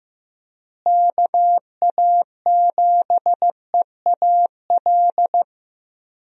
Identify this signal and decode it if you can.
Morse code, heard as a keyed tone, about 15 wpm, 700 Hz. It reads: KA7EAL
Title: USA Call Signs - 6 Characters - Repeated 3x - 15wpm